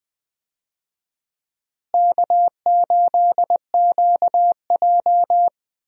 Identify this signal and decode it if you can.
Morse code, heard as a keyed tone, about 20 wpm, 700 Hz. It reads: K8QJ
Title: USA Call Signs - 4 Characters - Repeated 3x - 20wpm